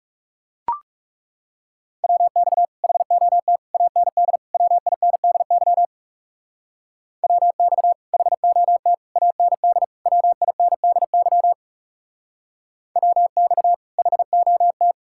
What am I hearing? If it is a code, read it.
Morse code, heard as a keyed tone, about 37 wpm, 700 Hz. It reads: WX HOT AND WINDY WX HOT AND WINDY WX HOT
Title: Speed-Racing Common QSO Phrases Encore - 30wpm